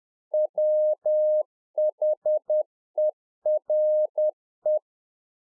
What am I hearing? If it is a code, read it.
Morse code, heard as a keyed tone, about 10 wpm, 610 Hz. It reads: WHERE